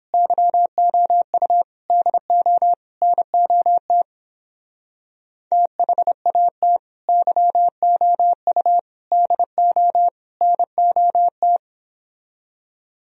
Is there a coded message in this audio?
Yes: Morse code from 0.1 to 11.6 s, heard as a keyed tone, about 31 wpm, 700 Hz. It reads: YOU DO NOT THAT YOU DO NOT